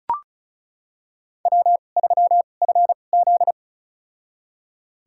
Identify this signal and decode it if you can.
Morse code, heard as a keyed tone, about 36 wpm, spaced as for 25 wpm, 700 Hz. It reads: W3FZ
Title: USA Call Signs - 4 Characters - 2x Character Spacing - 35wpm